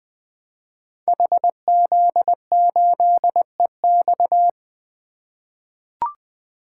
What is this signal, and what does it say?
Morse code, heard as a keyed tone, about 20 wpm, 700 Hz. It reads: HZ8EX
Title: Code Groups - 5 Letter:Numbers - 20wpm